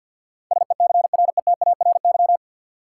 Morse code, heard as a keyed tone, about 53 wpm, 700 Hz. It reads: SECRETARY
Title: Sentences from Top 1300 Words - New Word Drill Alt - 50wpm